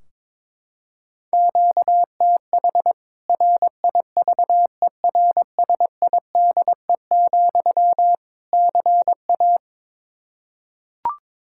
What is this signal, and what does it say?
Morse code, heard as a keyed tone, about 22 wpm, 700 Hz. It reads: QTH RIVERSIDE, CA
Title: K1USN SST QSO Practice 22wpm